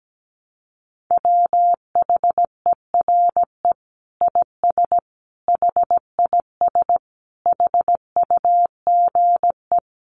Morse code, heard as a keyed tone, about 17 wpm, 700 Hz. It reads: WHERE IS HIS HUGE
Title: Sentences from Top 1100 Words 17wpm